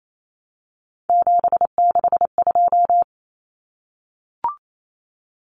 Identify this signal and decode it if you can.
Morse code, heard as a keyed tone, about 28 wpm, 700 Hz. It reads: 762
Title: Common Cut Numbers - 3-Digit 28wpm